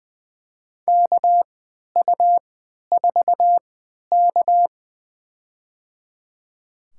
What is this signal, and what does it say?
Morse code, heard as a keyed tone, about 20 wpm, spaced as for 11 wpm, 700 Hz. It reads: KU4K
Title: USA Call Signs - 4 Characters - 3x Character Spacing - 20wpm